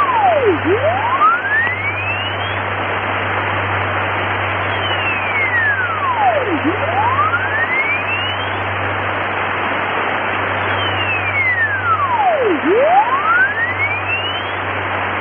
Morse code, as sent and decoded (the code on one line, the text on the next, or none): none